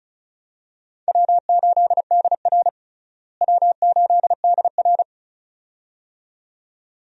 Morse code, heard as a keyed tone, about 36 wpm, 700 Hz. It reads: W8DR W8DR
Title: USA Call Signs - 4 Characters - Repeated 2x - 35wpm